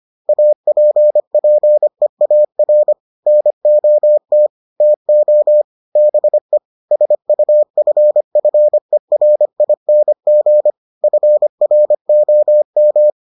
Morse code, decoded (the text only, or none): APPEAR NOT TO BE SUFFERING FROM